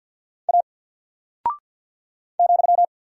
Morse code, heard as a keyed tone, about 50 wpm, 700 Hz. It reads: A ,
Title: Single Character (Letter:Number:Punctuation:Prosign) Alt 50wpm